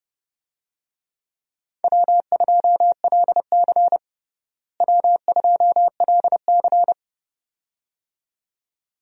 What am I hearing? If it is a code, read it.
Morse code, heard as a keyed tone, about 30 wpm, 700 Hz. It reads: W2LC W2LC
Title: USA Call Signs - 4 Characters - Repeated 2x - 30wpm